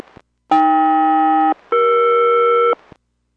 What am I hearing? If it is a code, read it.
Aircraft SELCAL, heard as two pairs of tones. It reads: AK-ER